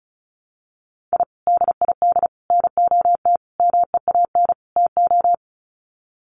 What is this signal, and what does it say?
Morse code, heard as a keyed tone, about 34 wpm, 700 Hz. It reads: I DID NOT MEAN TO
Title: Sets of 5 Words 35wpm